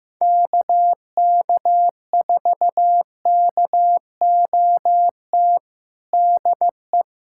Morse code, heard as a keyed tone, about 15 wpm, 700 Hz. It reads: KK4KOT DE
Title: USA Callsign DE Callsign 15wpm